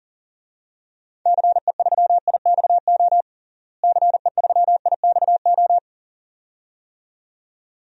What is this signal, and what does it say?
Morse code, heard as a keyed tone, about 37 wpm, 700 Hz. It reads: CE3IXO CE3IXO
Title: DXCC-100 Call Signs - 6 Characters - Repeated 2x - 40wpm